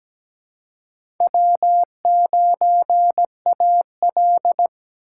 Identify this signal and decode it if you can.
Morse code, heard as a keyed tone, about 17 wpm, 700 Hz. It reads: W9AL
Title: USA Call Signs - 4 Characters - 4x Character Spacing - 17wpm